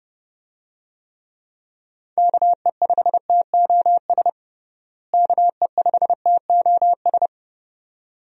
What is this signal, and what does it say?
Morse code, heard as a keyed tone, about 30 wpm, 700 Hz. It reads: KE5TOS KE5TOS
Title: USA Call Signs - 6 Characters - Repeated 2x - 30wpm